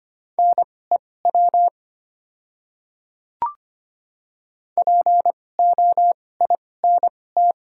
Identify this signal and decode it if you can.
Morse code, heard as a keyed tone, about 25 wpm, spaced as for 18 wpm, 700 Hz. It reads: NEW POINT
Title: 2x Character Spacing - Top 200 Words - 25wpm